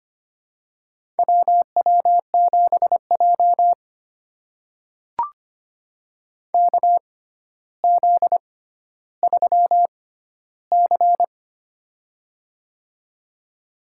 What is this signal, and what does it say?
Morse code, heard as a keyed tone, about 25 wpm, spaced as for 13 wpm, 700 Hz. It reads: WW7J KZ3C
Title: USA Call Signs - 4 Characters - 6x Character Spacing - 25wpm